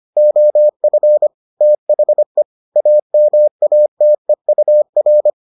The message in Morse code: OF THE AMATEUR

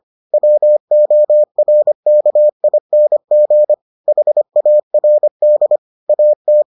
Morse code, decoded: WORKING HARD AT

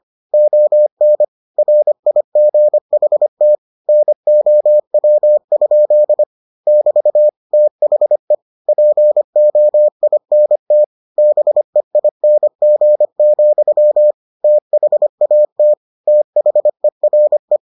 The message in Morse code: ON RIGHT NOW? <BT> THE POINT BEING, THAT THERE

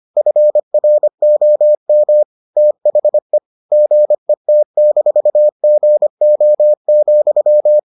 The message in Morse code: FROM THE GET-GO,

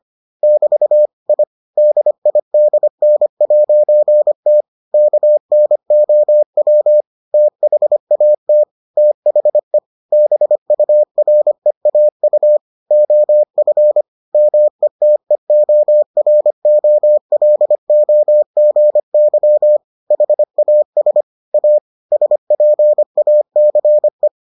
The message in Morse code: <BT> I DIDN'T KNOW THAT THE BUREAU OF METEOROLOGY HAS A SPACE

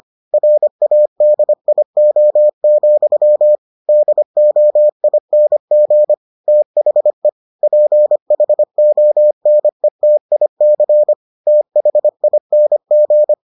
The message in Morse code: RADIO, DOING THE PHONETIC THING

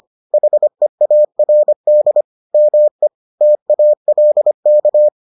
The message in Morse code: HEARD ME TALK

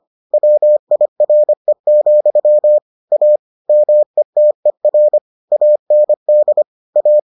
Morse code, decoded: WIRE, A METER AND A

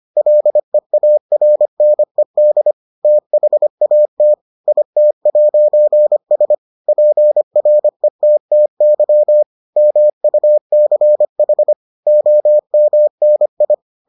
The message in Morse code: LEARNED THAT IT'S PRETTY MUCH OMNI

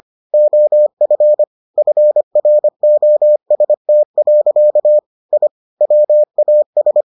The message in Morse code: OF FROST. I WAS

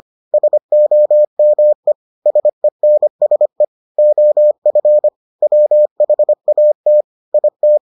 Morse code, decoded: SOME SENSE OF WHAT IT

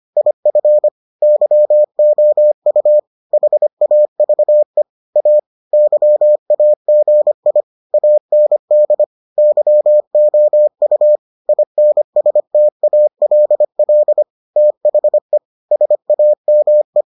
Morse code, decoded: IF YOU HAVE A YAGI AND YOU INSTALL THE SAME

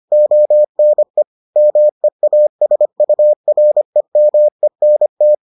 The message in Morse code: ONE MEASUREMENT

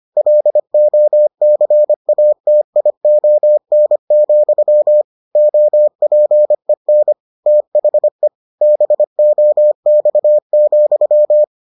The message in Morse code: LOCATION, OPEN THE BOX,

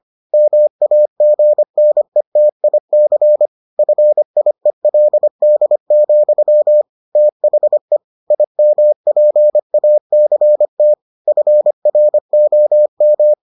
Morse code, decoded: MAGNETIC FIELD, THE IMPACT FROM

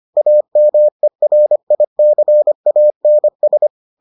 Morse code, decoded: AMERICANS

